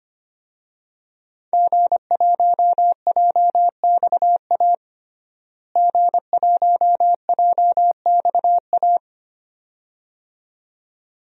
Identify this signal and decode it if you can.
Morse code, heard as a keyed tone, about 25 wpm, 700 Hz. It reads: G1JXA G1JXA
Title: DXCC-100 Call Signs - 5 Characters - Repeated 2x - 25wpm